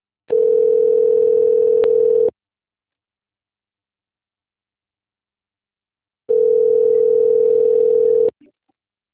Ringback tone (North America), 440 + 480 Hz.